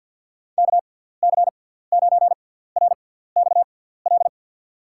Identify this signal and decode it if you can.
Morse code, heard as a keyed tone, about 52 wpm, spaced as for 17 wpm, 700 Hz. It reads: KC9RXL
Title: USA Call Signs - 6 Characters - 6x Character Spacing - 50wpm